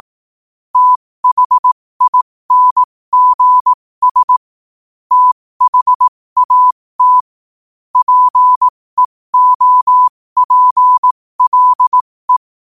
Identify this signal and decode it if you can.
Morse code, heard as a keyed tone, about 18 wpm, spaced as for 15 wpm, 1000 Hz. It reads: THINGS THAT PEOPLE
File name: qotdfiles / 15wpm